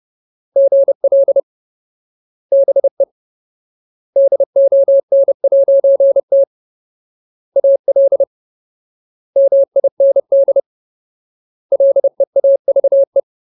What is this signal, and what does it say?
Morse code, heard as a keyed tone, about 30 wpm, 550 Hz. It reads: GL BE DON'T AL MIND LEAVE
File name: Ultimate Word List / 30WPM Ultimate